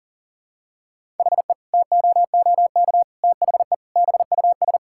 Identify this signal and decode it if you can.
Morse code, heard as a keyed tone, about 37 wpm, 700 Hz. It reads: HE TOOK THE BUS
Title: Sentences from Top 1000 Words - Review Part 6 - 40wpm